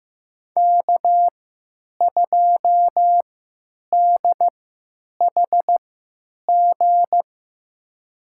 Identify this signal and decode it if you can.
Morse code, heard as a keyed tone, about 15 wpm, spaced as for 9 wpm, 700 Hz. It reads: K2DHG
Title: USA Call Signs - 5 Characters - 3x Character Spacing - 15wpm